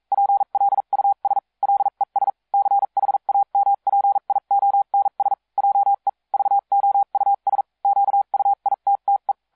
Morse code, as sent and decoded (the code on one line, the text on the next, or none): .--. .-.. ..- ... / .-.. ... / -.-. .... .- -- .--. .. --- -. ... / .--- . / ...- --- ..- ... / --.- ..- .. - - .
PLUS LES CHAMPIONS JE VOUS QUITTE